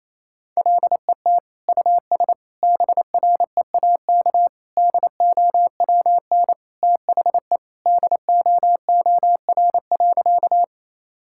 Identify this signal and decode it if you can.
Morse code, heard as a keyed tone, about 27 wpm, 700 Hz. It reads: LET US BREAK DOWN THE DOOR.
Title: Sentences from Top 1000 Words - Review Part 6 - 28wpm